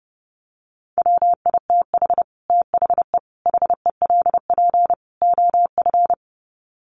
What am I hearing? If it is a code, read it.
Morse code, heard as a keyed tone, about 30 wpm, 700 Hz. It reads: WITH THE HELP OF
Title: Sets of 4 Words 30wpm